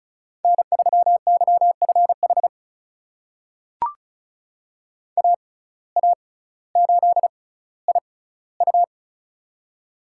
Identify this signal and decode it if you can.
Morse code, heard as a keyed tone, about 36 wpm, spaced as for 18 wpm, 700 Hz. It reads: N3YFH AA8IU
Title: USA Call Signs - 5 Characters - 6x Character Spacing - 35wpm